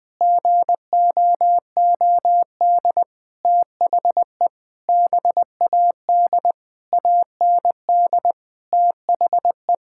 Morse code, heard as a keyed tone, about 20 wpm, 700 Hz. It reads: GOOD THE BAD AND THE
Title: Sentences with 7-Word Phrases - 20wpm